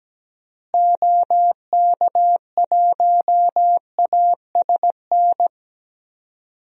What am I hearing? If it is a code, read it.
Morse code, heard as a keyed tone, about 17 wpm, 700 Hz. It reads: OK1ASN
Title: DXCC-100 Call Signs - 6 Characters - Repeated 3x - 17wpm